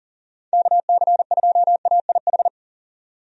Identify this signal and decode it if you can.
Morse code, heard as a keyed tone, about 38 wpm, 700 Hz. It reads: KC2AIH
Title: USA Call Signs - 6 Characters - 40wpm